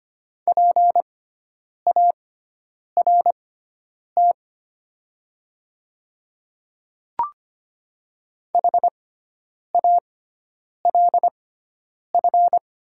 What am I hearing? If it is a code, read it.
Morse code, heard as a keyed tone, about 25 wpm, spaced as for 9 wpm, 700 Hz. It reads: PART HALF
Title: 6x Character Spacing - Top 500 Words - 25wpm